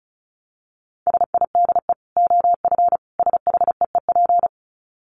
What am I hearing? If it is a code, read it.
Morse code, heard as a keyed tone, about 34 wpm, 700 Hz. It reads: SIDE OF SHEEP
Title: Dit Challenge 35wpm